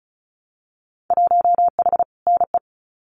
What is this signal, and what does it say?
Morse code, heard as a keyed tone, about 36 wpm, 690 Hz. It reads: 1H NE
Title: Winter Field Day Exchanges 35wpm